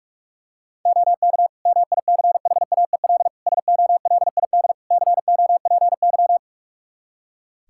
Morse code, heard as a keyed tone, about 46 wpm, 700 Hz. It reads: OK MICHAEL SOLID COPY
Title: Practice QSO - Exchange 3 - More Info - 45wpm